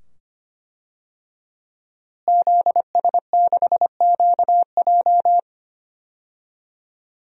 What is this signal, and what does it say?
Morse code, heard as a keyed tone, about 25 wpm, 700 Hz. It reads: ZS6QJ
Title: DXCC-100 Call Signs - 5 Characters - Repeated 2x - 25wpm